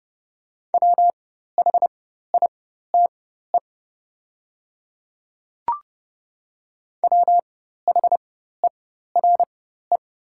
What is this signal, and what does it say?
Morse code, heard as a keyed tone, about 32 wpm, spaced as for 14 wpm, 700 Hz. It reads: WHITE WHERE
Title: 4x Character Spacing - Top 500 Words - 30wpm